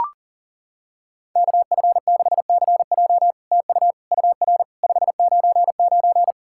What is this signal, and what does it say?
Morse code, heard as a keyed tone, about 40 wpm, 700 Hz. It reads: KF6CJ TU UR 599
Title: SOTA QSO Practice 40wpm